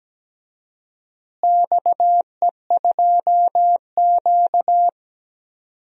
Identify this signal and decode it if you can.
Morse code, heard as a keyed tone, about 17 wpm, 700 Hz. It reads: XE2Q
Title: DXCC-100 Call Signs - 4 Characters - Repeated 3x - 17wpm